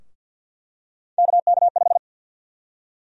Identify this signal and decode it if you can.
Morse code, heard as a keyed tone, about 47 wpm, 700 Hz. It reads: KK5
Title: USA Call Sign Prefixes - Repeated 2x - 50wpm